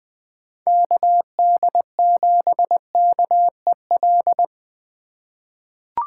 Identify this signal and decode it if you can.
Morse code, heard as a keyed tone, about 21 wpm, 700 Hz. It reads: KD7KEL
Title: USA Call Signs - 6 Characters - 2x Character Spacing - 20wpm